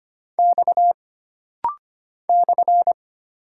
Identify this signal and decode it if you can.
Morse code, heard as a keyed tone, about 25 wpm, 700 Hz. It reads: X /
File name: Single Character (Letter:Number:Punctuation:Prosign) - Rapid-Fire - 25wpm